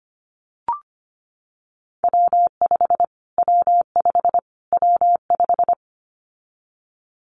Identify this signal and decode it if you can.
Morse code, heard as a keyed tone, about 25 wpm, 700 Hz. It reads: W5 W5 W5
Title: Callsign Prefixes for Top 25 DXCC - Repeated 3x - 25wpm